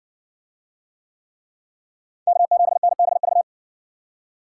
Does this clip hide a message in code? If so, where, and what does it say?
Yes: Morse code from 2.3 to 3.4 s, heard as a keyed tone, about 60 wpm, 700 Hz. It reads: K7NBV